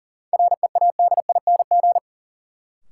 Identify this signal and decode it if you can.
Morse code, heard as a keyed tone, about 40 wpm, 700 Hz. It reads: READING
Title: Sentences from Top 600 Words - New Word Drill - 40wpm